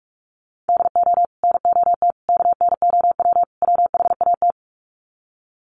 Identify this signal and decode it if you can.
Morse code, heard as a keyed tone, about 44 wpm, 700 Hz. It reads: DO NOT KNOW WHAT
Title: Sets of 4 Words 45wpm